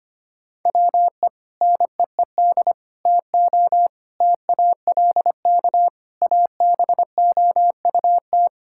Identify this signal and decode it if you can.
Morse code, heard as a keyed tone, about 25 wpm, 700 Hz. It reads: WE NEED TO TALK ABOUT
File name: practice / 25wpm / Sentences with 3-Word Phrases - 25wpm